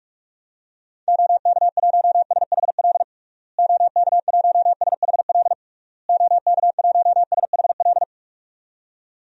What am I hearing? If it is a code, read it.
Morse code, heard as a keyed tone, about 46 wpm, 700 Hz. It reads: OK1SHL OK1SHL OK1SHL
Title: DXCC-100 Call Signs - 6 Characters - Repeated 3x - 45wpm